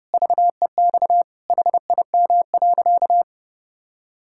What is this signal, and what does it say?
Morse code, heard as a keyed tone, about 30 wpm, 700 Hz. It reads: VEX HIM.
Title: Pangram Challenge v2 - 30wpm